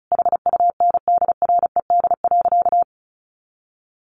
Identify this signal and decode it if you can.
Morse code, heard as a keyed tone, about 34 wpm, 700 Hz. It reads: HUNDRED.
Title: Sentences from Top 1300 Words 35wpm